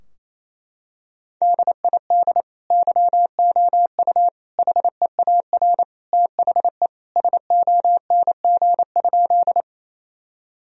Morse code, heard as a keyed tone, about 28 wpm, 700 Hz. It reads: DID YOU HEAR THE SONG?
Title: Sentences from Top 1000 Words - Review Part 2 - 28wpm